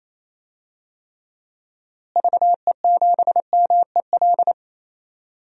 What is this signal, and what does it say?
Morse code, heard as a keyed tone, about 28 wpm, 700 Hz. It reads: VE7MEL
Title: DXCC-100 Call Signs - 6 Characters - Repeated 3x - 28wpm